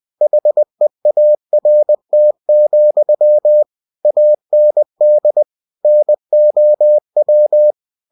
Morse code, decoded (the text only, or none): HEART, AND NOW